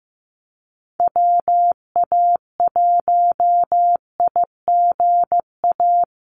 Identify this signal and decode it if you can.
Morse code, heard as a keyed tone, about 15 wpm, 700 Hz. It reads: WA1IGA
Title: USA Call Signs Repeated 3x 15wpm